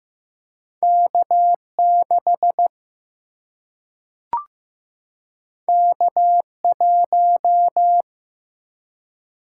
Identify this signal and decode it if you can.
Morse code, heard as a keyed tone, about 15 wpm, 700 Hz. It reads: K 6 K 1
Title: Single-Letter Single-Number 15wpm